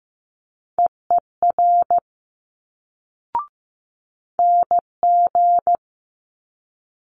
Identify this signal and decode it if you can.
Morse code, heard as a keyed tone, about 15 wpm, 700 Hz. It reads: EER NG
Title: Multi-letter Phonemes Quick Fire 15wpm